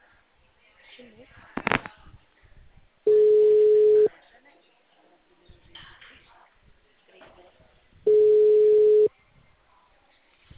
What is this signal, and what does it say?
Ringback tone (Europe, 425 Hz), with speech.
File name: out- -2015- - - .14757